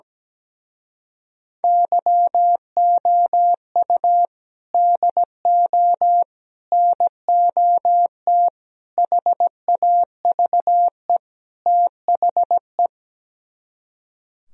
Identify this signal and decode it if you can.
Morse code, heard as a keyed tone, about 17 wpm, 700 Hz. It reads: YOU DO NOT HAVE THE
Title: Speed-Racing - Sets of 5 Words 17wpm